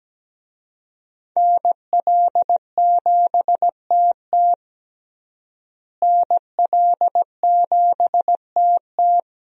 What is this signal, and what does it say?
Morse code, heard as a keyed tone, about 17 wpm, 700 Hz. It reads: NL7TT NL7TT
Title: USA Call Signs - 5 Characters - Repeated 2x - 17wpm